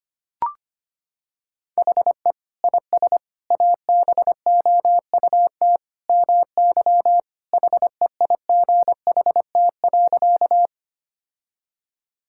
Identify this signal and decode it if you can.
Morse code, heard as a keyed tone, about 25 wpm, 700 Hz. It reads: HE IS ABOUT MY HEIGHT.
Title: Sentences from Top 1600 Words 25wpm